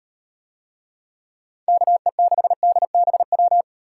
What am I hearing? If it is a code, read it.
Morse code, heard as a keyed tone, about 36 wpm, 700 Hz. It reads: KE6DBW